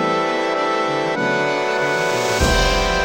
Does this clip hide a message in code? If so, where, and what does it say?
No.